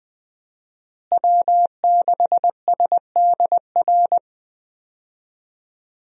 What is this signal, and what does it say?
Morse code, heard as a keyed tone, about 20 wpm, 700 Hz. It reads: W6SDR
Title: USA Call Signs - 5 Characters - Repeated 2x - 20wpm